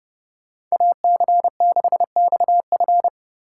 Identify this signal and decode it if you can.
Morse code, heard as a keyed tone, about 30 wpm, 700 Hz. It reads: AC6XF
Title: USA Call Signs - 5 Characters - 3x Character Spacing - 30wpm